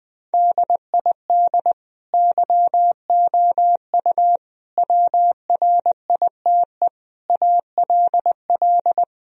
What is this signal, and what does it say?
Morse code, heard as a keyed tone, about 20 wpm, 700 Hz. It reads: DID YOU WRITE ALL
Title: Sentences from Top 1000 Words - Review Part 1 - 20wpm